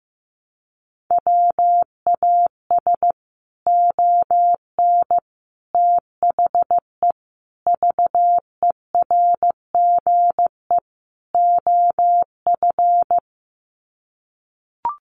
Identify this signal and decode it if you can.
Morse code, heard as a keyed tone, about 15 wpm, 700 Hz. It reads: WAS ON THE VERGE OF
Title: Sets of 5 Words Encore 15wpm